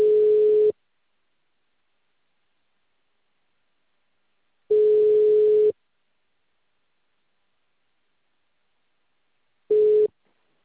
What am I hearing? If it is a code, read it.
Ringback tone (Europe, 425 Hz).